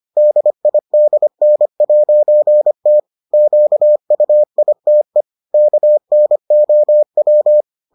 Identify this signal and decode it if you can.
Morse code, heard as a keyed tone, about 25 wpm, 600 Hz. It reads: DIDN'T QUITE KNOW